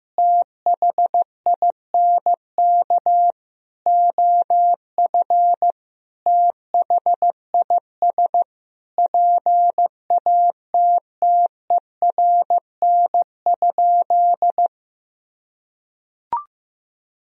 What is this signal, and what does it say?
Morse code, heard as a keyed tone, about 15 wpm, 700 Hz. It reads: THINK OF THIS PATTERN?
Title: Sentences from Top 1000 Words - Review Part 2 - 15wpm